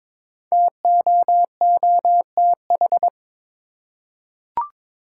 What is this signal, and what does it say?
Morse code, heard as a keyed tone, about 22 wpm, 700 Hz. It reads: TOOTH
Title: Five Letter Words - 22wpm